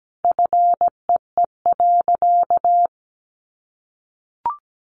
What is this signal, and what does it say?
Morse code, heard as a keyed tone, about 17 wpm, 700 Hz. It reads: FEE.
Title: Sentences from Top 1500 Words 17wpm